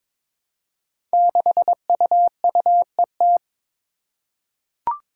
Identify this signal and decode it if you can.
Morse code, heard as a keyed tone, about 22 wpm, 700 Hz. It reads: 6UUET
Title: Five Challenging Letter-Number Combinations 22wpm